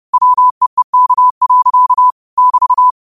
Morse code, heard as a keyed tone, about 30 wpm, 1000 Hz. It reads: WEEK. <BT>